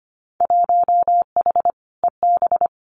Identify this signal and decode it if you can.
Morse code, heard as a keyed tone, about 25 wpm, 700 Hz. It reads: 1H EB
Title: Winter Field Day Exchanges 25wpm